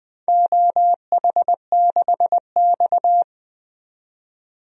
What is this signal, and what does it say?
Morse code, heard as a keyed tone, about 20 wpm, 700 Hz. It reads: OH6X
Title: DXCC-100 Call Signs - 4 Characters - 20wpm